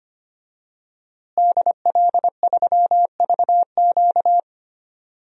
Morse code, heard as a keyed tone, about 25 wpm, 700 Hz. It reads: DL3VQ
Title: DXCC-100 Call Signs - 5 Characters - Repeated 2x - 25wpm